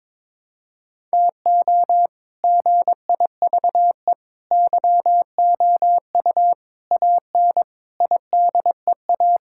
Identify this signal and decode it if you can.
Morse code, heard as a keyed tone, about 22 wpm, 700 Hz. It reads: TO GIVE YOU AN IDEA
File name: Sentences with 7-Word Phrases - 22wpm